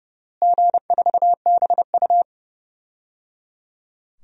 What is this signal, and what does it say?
Morse code, heard as a keyed tone, about 30 wpm, 700 Hz. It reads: G4BU
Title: DXCC-100 Call Signs - 4 Characters - Repeated 3x - 30wpm